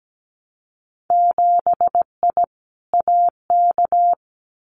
Morse code, heard as a keyed tone, about 17 wpm, 700 Hz. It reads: 7I AK
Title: Winter Field Day Exchanges Encore 17wpm